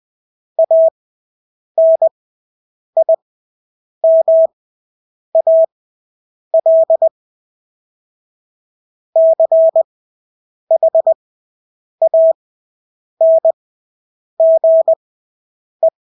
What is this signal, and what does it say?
Morse code, heard as a keyed tone, about 20 wpm, spaced as for 8 wpm, 650 Hz. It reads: ANIMAL CHANGE
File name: Engelske ord 20/8 wpm